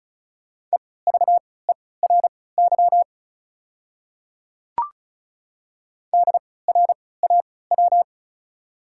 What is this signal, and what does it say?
Morse code, heard as a keyed tone, about 36 wpm, spaced as for 20 wpm, 690 Hz. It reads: EVERY DRAW
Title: 3x Character Spacing - Top 300 Words - 35wpm